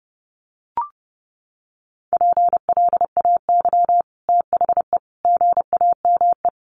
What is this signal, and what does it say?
Morse code, heard as a keyed tone, about 30 wpm, 700 Hz. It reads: PLAY THE GAME
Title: Sentences from Top 500 Words 30wpm